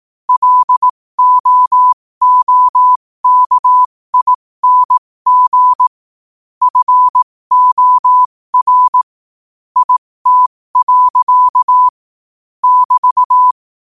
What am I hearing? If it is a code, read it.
Morse code, heard as a keyed tone, about 18 wpm, spaced as for 15 wpm, 1000 Hz. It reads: LOOKING FOR IT. <BT>
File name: qotdfiles / 15wpm